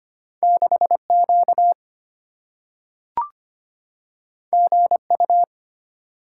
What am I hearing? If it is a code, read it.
Morse code, heard as a keyed tone, about 25 wpm, 700 Hz. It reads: 6Q GU
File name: Two Letter-Number Combinations - 25wpm